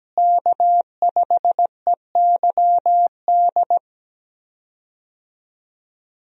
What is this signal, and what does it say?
Morse code, heard as a keyed tone, about 17 wpm, 700 Hz. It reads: K5EYD
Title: USA Call Signs - 5 Characters - Repeated 2x - 17wpm